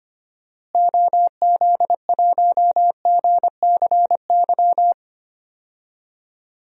Morse code, heard as a keyed tone, about 25 wpm, 700 Hz. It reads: OZ1GCY
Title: DXCC-100 Call Signs - 6 Characters - Repeated 3x - 25wpm